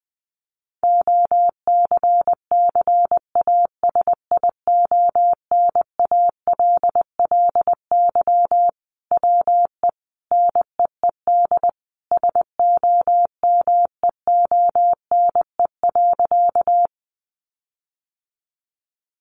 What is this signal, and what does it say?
Morse code, heard as a keyed tone, about 20 wpm, 700 Hz. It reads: OCCASIONALLY WE NEED SOMEONE.
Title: Sentences from Top 1300 Words 20wpm